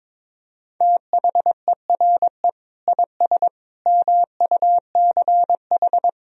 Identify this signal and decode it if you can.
Morse code, heard as a keyed tone, about 21 wpm, 700 Hz. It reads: THERE IS MUCH
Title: Sentences from Top 1000 Words - Review Part 4 - 22wpm